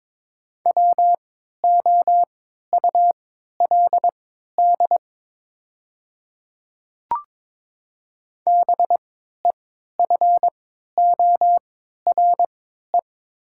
Morse code, heard as a keyed tone, about 22 wpm, spaced as for 12 wpm, 700 Hz. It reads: WOULD BEFORE
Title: 3x Character Spacing - Top 200 Words - 22wpm